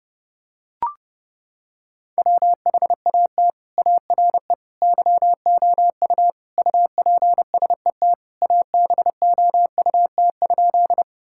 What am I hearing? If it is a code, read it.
Morse code, heard as a keyed tone, about 32 wpm, 700 Hz. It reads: WHAT ARE YOU UPSET ABOUT?
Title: Sentences from Top 1000 Words - Review Part 5 - 30wpm